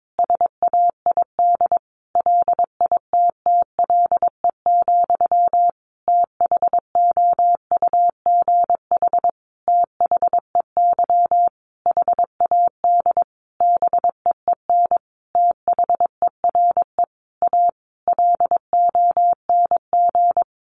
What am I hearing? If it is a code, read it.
Morse code, heard as a keyed tone, about 22 wpm, 700 Hz. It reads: SAID LITTLE, THOUGH THEY HAD BEEN THERE A LONG
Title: Sentences from Top 400 Words 22wpm